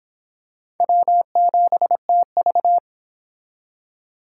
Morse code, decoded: W7TV

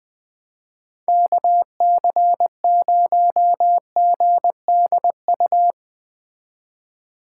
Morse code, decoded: KC0GDU

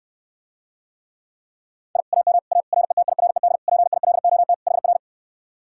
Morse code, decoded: I AM A RETIRED PILOT HR

